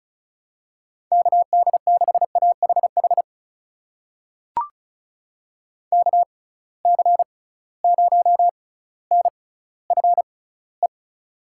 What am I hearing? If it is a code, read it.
Morse code, heard as a keyed tone, about 34 wpm, spaced as for 18 wpm, 700 Hz. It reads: KD6AHH KC0NFE
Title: USA Call Signs - 6x Character Spacing - 35wpm